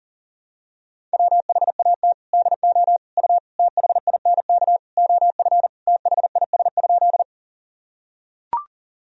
Morse code, heard as a keyed tone, about 40 wpm, 700 Hz. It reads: WHAT DO U THINK OF THIS?
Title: Sentences from Top 200 Words and Abbreviations 40wpm